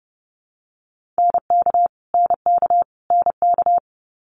Morse code, heard as a keyed tone, about 30 wpm, 700 Hz. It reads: NK NK NK
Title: Multi-letter Phonemes Repeated 3x 30wpm